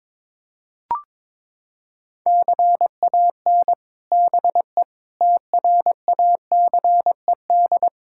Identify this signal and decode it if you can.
Morse code, heard as a keyed tone, about 22 wpm, 700 Hz. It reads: CAN BE TRACED
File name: practice / 22wpm / Sets of 6 Words - 22wpm